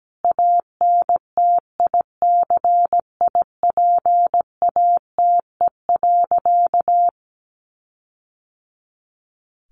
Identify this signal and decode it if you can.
Morse code, heard as a keyed tone, about 17 wpm, 700 Hz. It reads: ANTICIPATE.
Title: Sentences from Top 1500 Words 17wpm